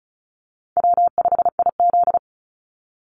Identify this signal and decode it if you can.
Morse code, heard as a keyed tone, about 34 wpm, 700 Hz. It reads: W5IZ
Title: Call Signs Repeated 3x - 35wpm